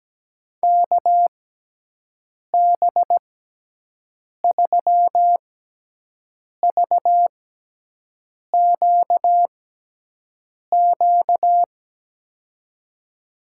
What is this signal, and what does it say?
Morse code, heard as a keyed tone, about 17 wpm, spaced as for 6 wpm, 700 Hz. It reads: KB3VQQ
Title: USA Call Signs - 6x Character Spacing - 17wpm